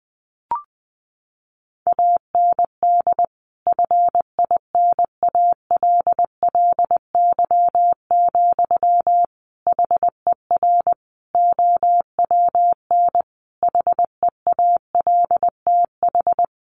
Morse code, heard as a keyed tone, about 20 wpm, 700 Hz. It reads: AND FINALLY, HER OWN HEALTH
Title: Sentences from Top 1100 Words 20wpm